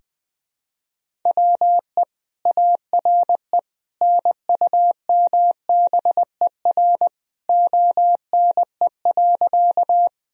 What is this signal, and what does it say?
Morse code, heard as a keyed tone, about 20 wpm, 700 Hz. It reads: WE ARE NUMBER ONE.
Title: Sentences from Top 1000 Words - Review Part 1 - 20wpm